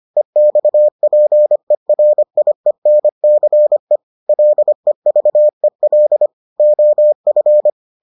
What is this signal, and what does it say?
Morse code, heard as a keyed tone, about 25 wpm, 600 Hz. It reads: EXPERIENCE LEVEL OF